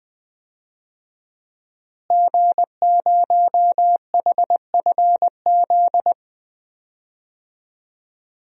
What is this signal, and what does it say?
Morse code, heard as a keyed tone, about 20 wpm, 700 Hz. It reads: G0HFZ
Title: DXCC-100 Call Signs - 5 Characters - Repeated 3x - 20wpm